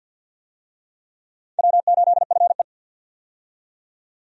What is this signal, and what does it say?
Morse code, heard as a keyed tone, about 46 wpm, 700 Hz. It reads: W8FE
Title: USA Call Signs - 4 Characters - 50wpm